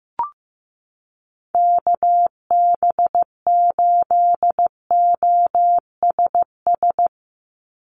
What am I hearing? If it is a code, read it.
Morse code, heard as a keyed tone, about 15 wpm, 700 Hz. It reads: KB8OSS
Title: USA Call Signs 15wpm